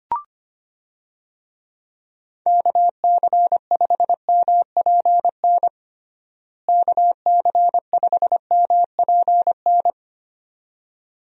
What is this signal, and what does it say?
Morse code, heard as a keyed tone, about 25 wpm, 700 Hz. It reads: KC5MPN KC5MPN
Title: USA Call Signs - 6 Characters - Repeated 2x - 25wpm